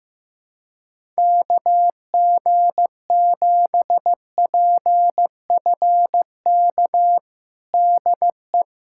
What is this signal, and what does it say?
Morse code, heard as a keyed tone, about 16 wpm, 700 Hz. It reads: KG7PFK DE
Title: USA Callsign DE Callsign - 6 Characters - 15wpm